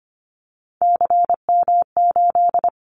Morse code, heard as a keyed tone, about 25 wpm, 700 Hz. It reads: CM8
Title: Callsign Prefixes for Top 100 DXCC - 25wpm